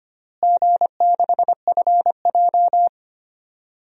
Morse code, heard as a keyed tone, about 25 wpm, 700 Hz. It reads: G6FJ
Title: DXCC-100 Call Signs - 4 Characters - Repeated 3x - 25wpm